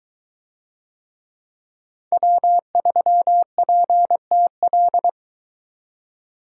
Morse code, heard as a keyed tone, about 22 wpm, 700 Hz. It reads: W3PTL